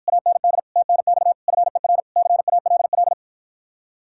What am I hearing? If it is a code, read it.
Morse code, heard as a keyed tone, about 54 wpm, 700 Hz. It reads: AND TNX FER CALL